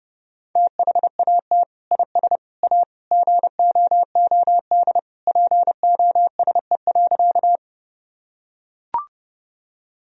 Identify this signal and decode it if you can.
Morse code, heard as a keyed tone, about 30 wpm, 700 Hz. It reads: THAT IS A GOOD POSE.
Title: Sentences from Top 1000 Words - Review Part 2 - 30wpm